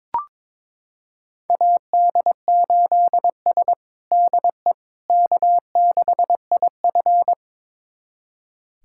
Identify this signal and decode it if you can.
Morse code, heard as a keyed tone, about 22 wpm, 700 Hz. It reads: AD8S DE K6IF
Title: USA Callsign DE Callsign - 4 Characters - 22wpm